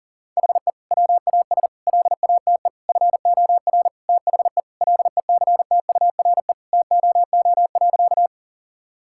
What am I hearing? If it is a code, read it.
Morse code, heard as a keyed tone, about 40 wpm, 700 Hz. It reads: HE WAS LATE FOR THE LECTURE TOO.
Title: Sentences from Top 1600 Words 40wpm